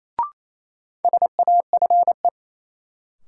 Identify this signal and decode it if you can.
Morse code, heard as a keyed tone, about 28 wpm, 700 Hz. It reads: SAFE
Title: Sentences from Top 900 Words - New Word Drill - 28wpm